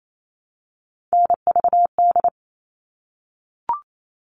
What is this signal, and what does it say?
Morse code, heard as a keyed tone, about 28 wpm, 700 Hz. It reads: NVD
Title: All Cut Numbers - 3-Digit 28wpm